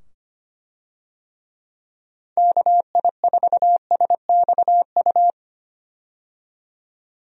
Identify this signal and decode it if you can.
Morse code, heard as a keyed tone, about 25 wpm, 700 Hz. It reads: KI4SXU
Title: USA Call Signs - 6 Characters - 25wpm